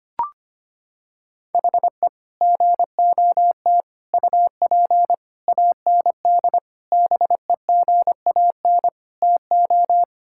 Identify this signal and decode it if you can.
Morse code, heard as a keyed tone, about 25 wpm, 700 Hz. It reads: HE GOT UP AND BEGAN TO
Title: Sentences from Top 1600 Words 25wpm